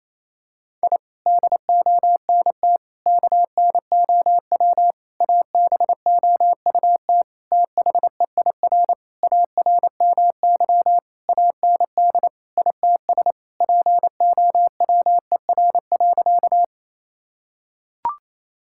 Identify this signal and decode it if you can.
Morse code, heard as a keyed tone, about 28 wpm, 700 Hz. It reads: I DONT KNOW ABOUT THEIR ARMY AND ITS POWER.
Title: Sentences from Top 1000 Words - Review Part 5 - 28wpm